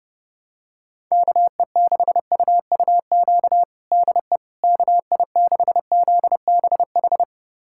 Morse code, heard as a keyed tone, about 30 wpm, 700 Hz. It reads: KE6UUQ DE KI6ZBH
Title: USA Callsign DE Callsign 30wpm